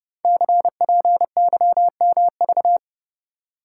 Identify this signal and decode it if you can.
Morse code, heard as a keyed tone, about 30 wpm, 700 Hz. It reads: CPYMV
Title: Code Groups - 5 Letter:Numbers - 30wpm